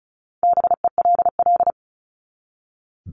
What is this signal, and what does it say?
Morse code, heard as a keyed tone, about 34 wpm, 700 Hz. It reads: BELL
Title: Top 1751-2000 Words 35wpm